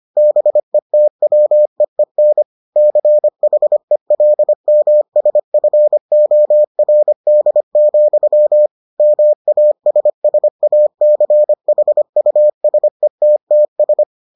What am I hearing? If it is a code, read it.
Morse code, heard as a keyed tone, about 25 wpm, 600 Hz. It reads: BETWEEN CHELMSFORD, MASSACHUSETTS